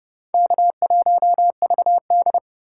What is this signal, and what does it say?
Morse code, heard as a keyed tone, about 36 wpm, 700 Hz. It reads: K1VD
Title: Speed Racing - USA 1x2 Call Signs 30wpm